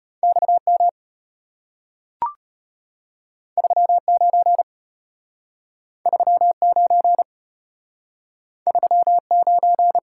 Morse code, heard as a keyed tone, about 34 wpm, 700 Hz. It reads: XM 39 39 39